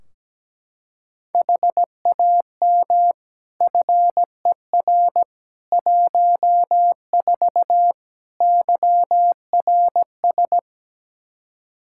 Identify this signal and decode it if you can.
Morse code, heard as a keyed tone, about 17 wpm, 700 Hz. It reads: HAM FER 14 YRS
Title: Practice QSO - Exchange 3 - More Info - 17wpm